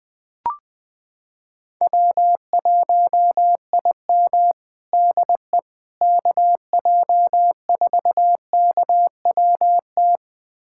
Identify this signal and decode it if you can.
Morse code, heard as a keyed tone, about 20 wpm, 700 Hz. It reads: W1IM DE KJ4KWT
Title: USA Callsign DE Callsign 20wpm